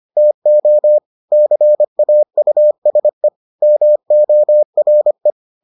Morse code, heard as a keyed tone, about 25 wpm, 600 Hz. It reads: TO CAUSE MORE